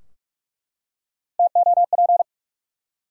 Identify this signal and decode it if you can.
Morse code, heard as a keyed tone, about 46 wpm, 700 Hz. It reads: TOP
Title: Three Letter Words - 45wpm